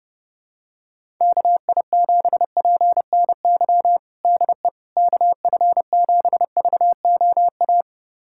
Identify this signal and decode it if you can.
Morse code, heard as a keyed tone, about 30 wpm, 700 Hz. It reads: KI7PNY DE KF7VOA
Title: USA Callsign DE Callsign - 6 Characters - 30wpm